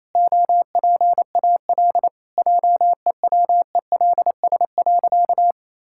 Morse code, heard as a keyed tone, about 28 wpm, 700 Hz. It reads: OPAL JEWELS.
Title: Pangram Challenge v2 - 28wpm